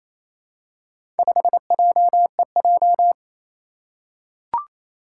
Morse code, heard as a keyed tone, about 28 wpm, 700 Hz. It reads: 5JEJ
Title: Four Challenging Letter-Number Combinations 28wpm